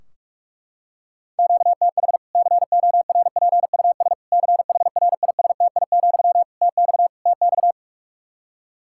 Morse code, heard as a keyed tone, about 41 wpm, 700 Hz. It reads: QTH CORPUS CHRISTI, TX TX
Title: K1USN SST QSO Practice 45wpm